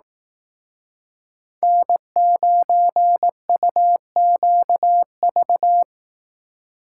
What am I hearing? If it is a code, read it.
Morse code, heard as a keyed tone, about 18 wpm, 700 Hz. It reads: N9UQV
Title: Speed Racing - USA 1x3 Call Signs 15wpm